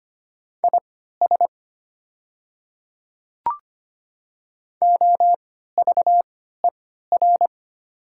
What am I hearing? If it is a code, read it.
Morse code, heard as a keyed tone, about 25 wpm, spaced as for 14 wpm, 700 Hz. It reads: IS OVER
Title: 3x Character Spacing - Top 100 Words - 25wpm